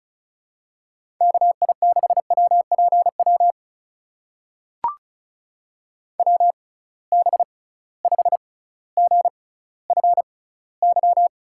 Morse code, heard as a keyed tone, about 36 wpm, spaced as for 18 wpm, 700 Hz. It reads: KI6WPW WB5GFY
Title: USA Call Signs - 6 Characters - 6x Character Spacing - 35wpm